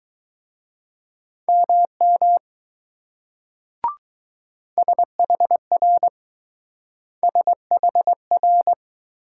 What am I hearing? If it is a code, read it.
Morse code, heard as a keyed tone, about 20 wpm, 700 Hz. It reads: MM SHR SHR